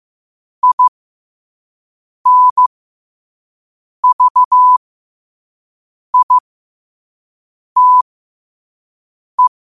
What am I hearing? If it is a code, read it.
Morse code, heard as a keyed tone, about 15 wpm, spaced as for 5 wpm, 1000 Hz. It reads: INVITE
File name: qotdfiles / 5wpm